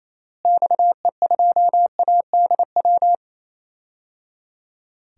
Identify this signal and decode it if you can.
Morse code, heard as a keyed tone, about 28 wpm, 700 Hz. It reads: XE2ADW
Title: DXCC-100 Call Signs - 6 Characters - Repeated 3x - 28wpm